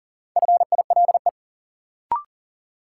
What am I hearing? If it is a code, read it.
Morse code, heard as a keyed tone, about 39 wpm, 700 Hz. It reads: FILE
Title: Sentences from Top 1200 Words - New Word Drill Alt - 40wpm